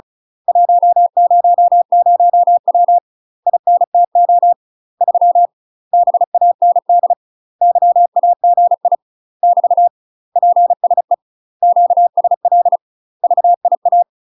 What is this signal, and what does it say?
Morse code, heard as a keyed tone, about 36 wpm, 700 Hz. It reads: 100W INTO 3 BAND YAGI <BT> PSE QSL VIA
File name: UK MASTER QSOs TRAINING/35WPM